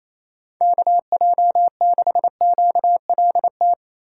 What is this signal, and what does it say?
Morse code, heard as a keyed tone, about 28 wpm, 700 Hz. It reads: KJ6QLT
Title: USA Call Signs - 6 Characters - 2x Character Spacing - 28wpm